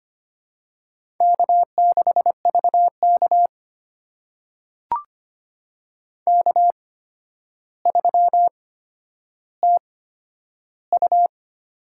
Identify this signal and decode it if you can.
Morse code, heard as a keyed tone, about 25 wpm, spaced as for 11 wpm, 700 Hz. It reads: K6VK K3TU
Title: USA Call Signs - 4 Characters - 8x Character Spacing - 25wpm